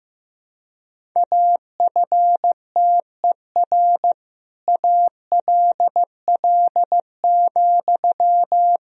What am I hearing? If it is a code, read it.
Morse code, heard as a keyed tone, about 15 wpm, 700 Hz. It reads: AFTER ALL,
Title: Sentences from Top 1000 Words - Review Part 2 - 15wpm